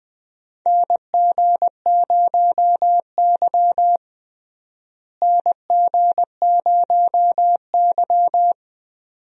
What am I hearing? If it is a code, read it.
Morse code, heard as a keyed tone, about 20 wpm, 700 Hz. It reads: NG0Y NG0Y
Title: USA Call Signs - 4 Characters - Repeated 2x - 20wpm